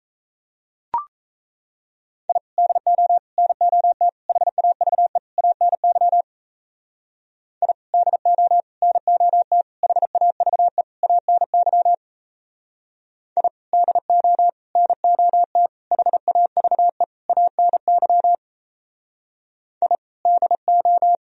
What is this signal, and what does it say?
Morse code, heard as a keyed tone, about 36 wpm, 700 Hz. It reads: I DO NOT HAVE ANY I DO NOT HAVE ANY I DO NOT HAVE ANY I DO